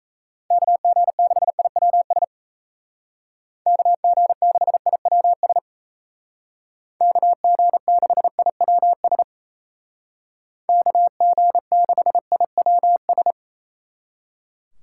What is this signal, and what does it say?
Morse code, heard as a keyed tone, about 34 wpm, 700 Hz. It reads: KG6IWS KG6IWS KG6IWS KG6IWS
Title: Speed Racing - USA Call Signs 28wpm